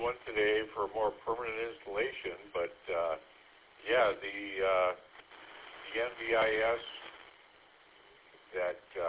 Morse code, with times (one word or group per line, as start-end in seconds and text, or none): none